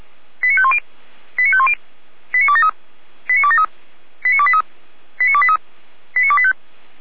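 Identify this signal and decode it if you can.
Five-tone selective call (ZVEI), heard as quick runs of five tones, tones of about 70 ms. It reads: 89510 89510 89273 89273 89283 89283 89285
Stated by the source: ZVEI five-tone selective call received by SDR, feuerwehr-orlamuende.org fire brigade Orlamünde.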